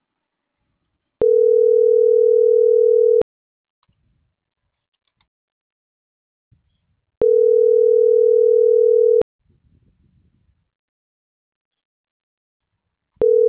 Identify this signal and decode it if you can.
Ringback tone (North America), 440 + 480 Hz.